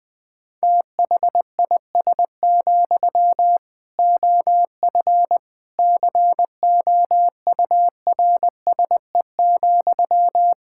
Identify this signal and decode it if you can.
Morse code, heard as a keyed tone, about 20 wpm, 700 Hz. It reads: THIS, OF COURSE,
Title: Sentences from Top 1600 Words 20wpm